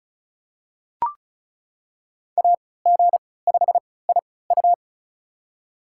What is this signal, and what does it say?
Morse code, heard as a keyed tone, about 36 wpm, spaced as for 20 wpm, 700 Hz. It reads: AG5IU
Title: USA Call Signs - 5 Characters - 3x Character Spacing - 35wpm